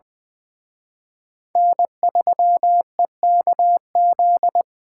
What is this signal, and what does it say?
Morse code, heard as a keyed tone, about 23 wpm, 700 Hz. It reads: N3EKZ
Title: Speed Racing - USA 1x3 Call Signs 20wpm